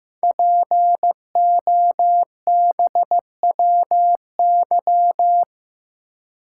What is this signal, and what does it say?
Morse code, heard as a keyed tone, about 15 wpm, 700 Hz. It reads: POBWY
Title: Code Groups - 5 Letters - 15wpm